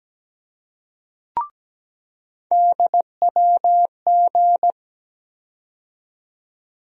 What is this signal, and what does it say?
Morse code, heard as a keyed tone, about 17 wpm, 700 Hz. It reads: DWG
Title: USA Call Sign Postfixes - 3 Letters 17wpm